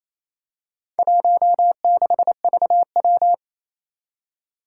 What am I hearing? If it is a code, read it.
Morse code, heard as a keyed tone, about 29 wpm, 700 Hz. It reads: 16VW
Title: Four Challenging Letter-Number Combinations 28wpm